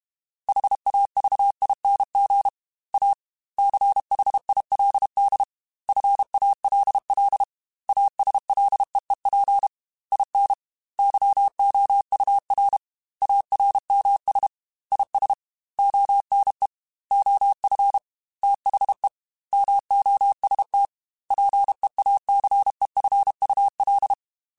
Morse code, heard as a keyed tone, about 32 wpm, 800 Hz. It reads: HAVING A CHILD FALL ASLEEP IN YOUR ARMS IS ONE OF THE MOST PEACEFUL
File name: Quotes / Quotes-mp3s / ChildAsleep @30